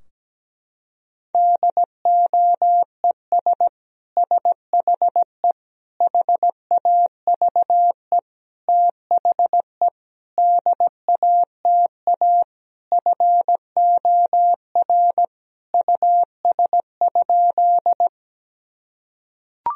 Morse code, heard as a keyed tone, about 17 wpm, 700 Hz. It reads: DOES SHE HAVE THE DATA FOR US?
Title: Sentences from Top 1000 Words - Review Part 3 - 17wpm